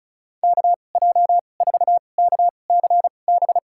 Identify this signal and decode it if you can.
Morse code, heard as a keyed tone, about 36 wpm, spaced as for 25 wpm, 700 Hz. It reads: KJ4KCB
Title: USA Call Signs - 6 Characters - 2x Character Spacing - 35wpm